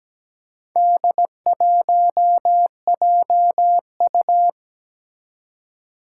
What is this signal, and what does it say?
Morse code, heard as a keyed tone, about 17 wpm, 700 Hz. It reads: D1JU
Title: Four Challenging Letter-Number Combinations 17wpm